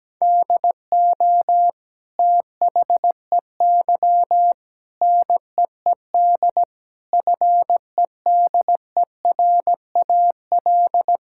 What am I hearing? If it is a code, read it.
Morse code, heard as a keyed tone, about 17 wpm, 700 Hz. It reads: DO THEY NEED FEDERAL
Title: Sentences from Top 1000 Words - Review Part 5 - 17wpm